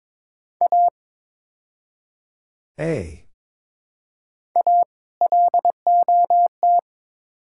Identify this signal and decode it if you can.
Morse code, heard as a keyed tone, about 22 wpm, 700 Hz. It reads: A A LOT
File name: practice / 22wpm / Build A Phrase - 5-Word Phrases - 22wpm